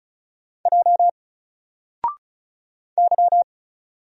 Morse code, heard as a keyed tone, about 35 wpm, 700 Hz. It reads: J Y
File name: Single-Letter-Number-35wpm